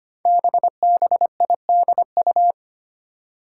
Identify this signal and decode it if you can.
Morse code, heard as a keyed tone, about 25 wpm, 700 Hz. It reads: BBIDU
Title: Five Challenging Letter-Number Combinations 25wpm